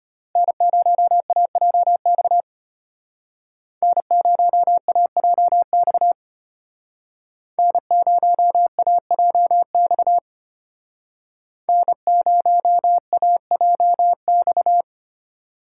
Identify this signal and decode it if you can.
Morse code, heard as a keyed tone, about 32 wpm, 700 Hz. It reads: N0AJX N0AJX N0AJX N0AJX